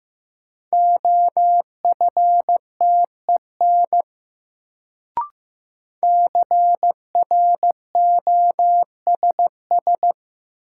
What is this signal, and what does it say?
Morse code, heard as a keyed tone, about 15 wpm, 700 Hz. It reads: OFTEN CROSS
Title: Sentences from Top 300 Words - New Word Drill - 15wpm